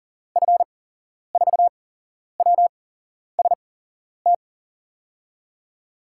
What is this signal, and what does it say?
Morse code, heard as a keyed tone, about 40 wpm, spaced as for 11 wpm, 700 Hz. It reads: F4WST
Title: USA Call Signs - 6 Characters - 8x Character Spacing - 40wpm